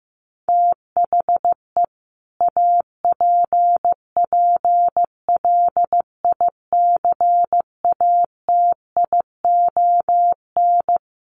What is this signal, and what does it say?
Morse code, heard as a keyed tone, about 15 wpm, 700 Hz. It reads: THE APPLICATION
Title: Sentences from Top 1200 Words 15wpm